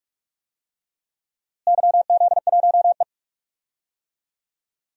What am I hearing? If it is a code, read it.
Morse code, heard as a keyed tone, about 41 wpm, 700 Hz. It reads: YZ1E
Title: DXCC-100 Call Signs - 4 Characters - Repeated 2x - 45wpm